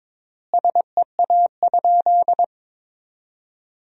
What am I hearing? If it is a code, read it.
Morse code, heard as a keyed tone, about 22 wpm, 700 Hz. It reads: SEA?
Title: 3x Word-Spacing - Sentences from Top 300 Words - 22wpm